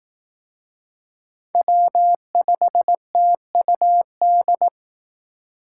Morse code, heard as a keyed tone, about 18 wpm, 700 Hz. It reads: W5TUD
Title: Speed Racing - USA 1x3 Call Signs 15wpm